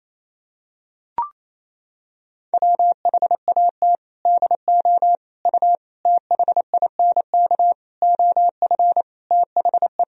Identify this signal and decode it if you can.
Morse code, heard as a keyed tone, about 28 wpm, 700 Hz. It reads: WHAT DO U THINK OF THE
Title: Sentences from Top 300 Words and Abbreviations 28wpm